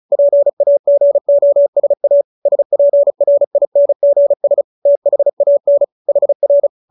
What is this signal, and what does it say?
Morse code, heard as a keyed tone, about 35 wpm, 570 Hz. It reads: PAGOSA SPRINGS THAN HR
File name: QSO312 35